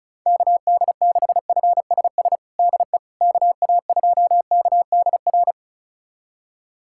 Morse code, heard as a keyed tone, about 34 wpm, 700 Hz. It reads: KD6FSS DE KA2KDR
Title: USA Callsign DE Callsign 35wpm